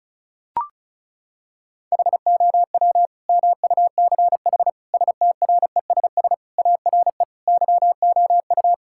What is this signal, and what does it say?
Morse code, heard as a keyed tone, about 34 wpm, 700 Hz. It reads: HOW MUCH STRESS ARE YOU
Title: Sentences from Top 1000 Words - Review Part 6 - 35wpm